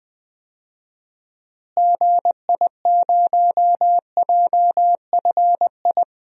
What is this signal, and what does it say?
Morse code, heard as a keyed tone, about 20 wpm, 700 Hz. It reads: GI0JFI
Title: DXCC-100 Call Signs - 6 Characters - Repeated 3x - 20wpm